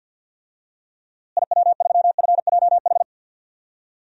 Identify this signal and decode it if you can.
Morse code, heard as a keyed tone, about 50 wpm, 700 Hz. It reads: IW3FJH
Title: DXCC-100 Call Signs - 6 Characters - Repeated 3x - 50wpm